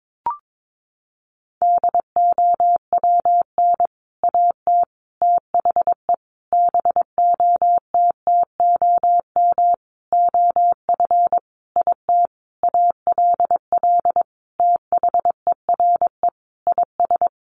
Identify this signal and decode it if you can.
Morse code, heard as a keyed tone, about 22 wpm, 700 Hz. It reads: DOWN AT THE BOTTOM OF IT ALL THERE IS
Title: Sentences from Top 1100 Words 22wpm